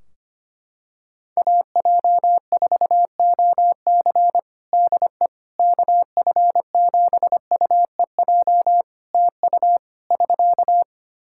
Morse code, heard as a keyed tone, about 25 wpm, 700 Hz. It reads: AJ4OC DE KF7UEJ TU <SK>
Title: Practice QSO - Exchange 4 - Ending - 25wpm